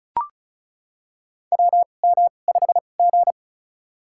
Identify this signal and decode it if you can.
Morse code, heard as a keyed tone, about 36 wpm, spaced as for 25 wpm, 690 Hz. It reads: WM5G
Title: USA Call Signs - 4 Characters - 2x Character Spacing - 35wpm